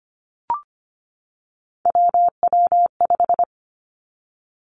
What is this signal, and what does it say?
Morse code, heard as a keyed tone, about 25 wpm, 700 Hz. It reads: WW5
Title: Three Challenging Letter-Number Combinations 25wpm